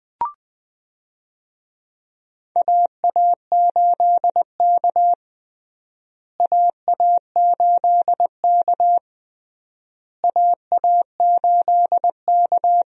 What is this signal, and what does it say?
Morse code, heard as a keyed tone, about 20 wpm, 700 Hz. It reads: AA8K AA8K AA8K
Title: USA Call Signs - 4 Characters - Repeated 3x - 20wpm